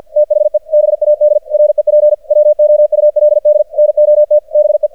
Morse code, heard as a keyed tone, about 48 wpm, 600 Hz. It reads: THE BAD REP WOULD NOT BE
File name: archived / CRSnet